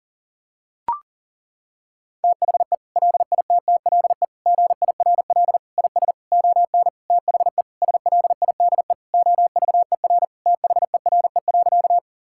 Morse code, heard as a keyed tone, about 37 wpm, 700 Hz. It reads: THE LITTLE GIRL IS ON THE SLIDE OVER THERE.
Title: Sentences from Top 1600 Words 40wpm